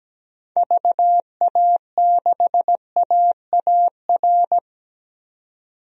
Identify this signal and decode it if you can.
Morse code, heard as a keyed tone, about 17 wpm, 700 Hz. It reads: VA6AAR
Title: DXCC-100 Call Signs - 6 Characters - 17wpm